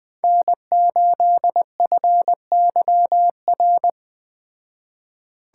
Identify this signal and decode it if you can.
Morse code, heard as a keyed tone, about 20 wpm, 700 Hz. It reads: N8FYR
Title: Speed Racing - USA 1x3 Call Signs 17wpm